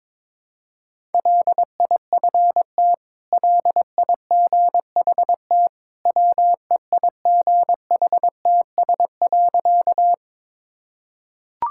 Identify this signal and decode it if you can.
Morse code, heard as a keyed tone, about 21 wpm, 700 Hz. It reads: LIFT LIGHT WEIGHTS.
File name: practice / 22wpm / Easy Sentences with Hints - 22wpm